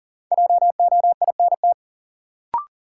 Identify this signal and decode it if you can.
Morse code, heard as a keyed tone, about 40 wpm, 700 Hz. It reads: JOINT
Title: Sets of 5 Words Encore - New Word Drill - 40wpm